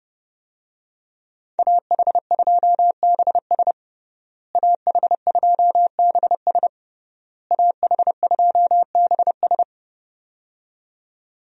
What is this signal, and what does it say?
Morse code, heard as a keyed tone, about 29 wpm, 700 Hz. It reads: AH2BS AH2BS AH2BS
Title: USA Call Signs - 5 Characters - Repeated 3x - 30wpm